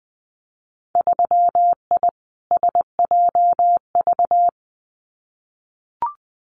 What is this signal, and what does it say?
Morse code, heard as a keyed tone, about 20 wpm, 700 Hz. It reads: 3I SJV
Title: Winter Field Day Exchanges Encore 20wpm